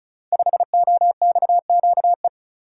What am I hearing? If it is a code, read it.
Morse code, heard as a keyed tone, about 36 wpm, 700 Hz. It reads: 5OXQE
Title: Code Groups - 5 Letter:Numbers - 35wpm